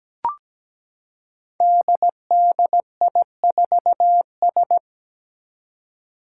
Morse code, heard as a keyed tone, about 17 wpm, 700 Hz. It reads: DDI4S
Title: Five Challenging Letter-Number Combinations 17wpm